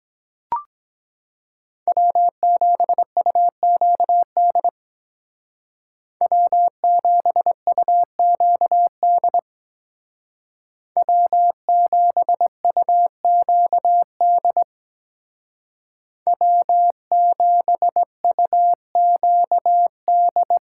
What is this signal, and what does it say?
Morse code, heard as a keyed tone, about 20 wpm, 700 Hz. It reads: W7UQD W7UQD W7UQD W7UQD
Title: Speed Racing - USA 1x3 Call Signs 17wpm